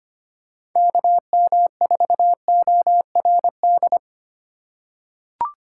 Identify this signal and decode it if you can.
Morse code, heard as a keyed tone, about 25 wpm, 700 Hz. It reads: KM4ORD
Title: USA Call Signs - 6 Characters - 4x Character Spacing - 25wpm